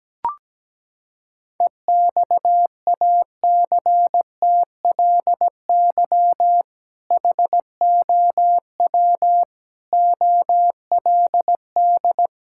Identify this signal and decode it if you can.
Morse code, heard as a keyed tone, about 17 wpm, 700 Hz. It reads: EXACTLY HOW OLD
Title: Sentences from Top 1600 Words 17wpm